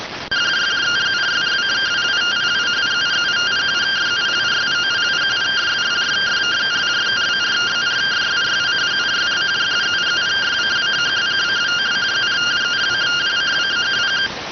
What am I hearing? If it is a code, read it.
Two-tone teleprinter signal, about 170 Hz shift.
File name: findings / MR CORONAL 55.gen-11 rx